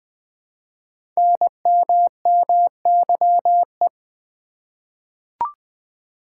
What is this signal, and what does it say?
Morse code, heard as a keyed tone, about 20 wpm, 700 Hz. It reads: NMMYE
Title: Code Groups - 5 Letters - 20wpm